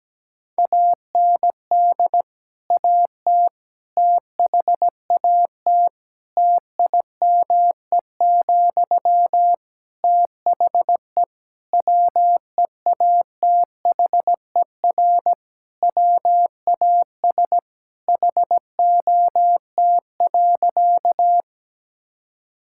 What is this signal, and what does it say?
Morse code, heard as a keyed tone, about 17 wpm, 700 Hz. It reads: AND AT THAT TIME, THE WEATHER WAS HOT.
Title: Sentences from Top 1000 Words - Review Part 5 - 17wpm